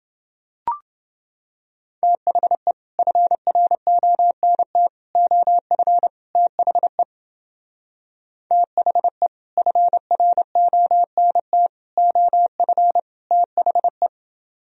Morse code, heard as a keyed tone, about 28 wpm, 700 Hz. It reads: THE FRONT OF THE THE FRONT OF THE